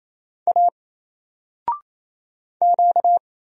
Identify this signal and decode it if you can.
Morse code, heard as a keyed tone, about 27 wpm, 700 Hz. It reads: A Q
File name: Single-Letter-Number-28wpm